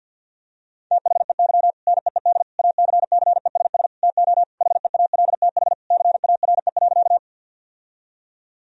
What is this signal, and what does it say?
Morse code, heard as a keyed tone, about 50 wpm, 700 Hz. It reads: THEY NEED ACCESS TO HEALTH CARE.
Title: Sentences from Top 1000 Words - Review Part 4 - 50wpm